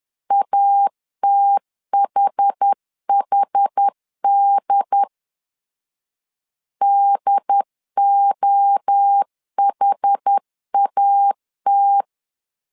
Morse code, heard as a keyed tone, about 10 wpm, 800 Hz. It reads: ATHHD DOHAT
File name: CW / DET / 1 3 1500C S A 10wpm 3447